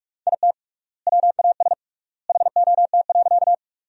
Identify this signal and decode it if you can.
Morse code, heard as a keyed tone, about 44 wpm, 700 Hz. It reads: IT WAS HOT.
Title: 3x Word-Spacing - Sentences from Top 500 Words - 45wpm